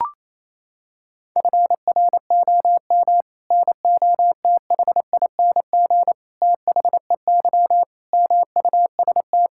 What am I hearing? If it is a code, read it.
Morse code, heard as a keyed tone, about 28 wpm, 700 Hz. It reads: FROM NOTHING THEY MUST
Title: Sentences from Top 1000 Words - Review Part 5 - 28wpm